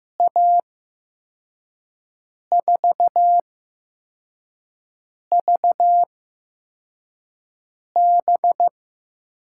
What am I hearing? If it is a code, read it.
Morse code, heard as a keyed tone, about 15 wpm, spaced as for 4 wpm, 700 Hz. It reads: A4VB
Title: USA Call Signs - 8x Character Spacing - 15wpm